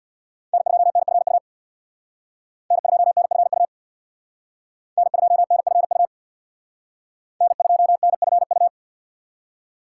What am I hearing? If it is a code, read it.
Morse code, heard as a keyed tone, about 62 wpm, 700 Hz. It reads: N2NFU N2NFU N2NFU N2NFU
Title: Speed Racing - USA Call Signs 50wpm